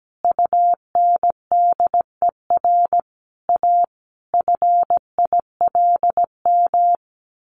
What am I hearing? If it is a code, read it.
Morse code, heard as a keyed tone, about 17 wpm, 700 Hz. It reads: UNDER A FILM
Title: A Sound Of Thunder 17wpm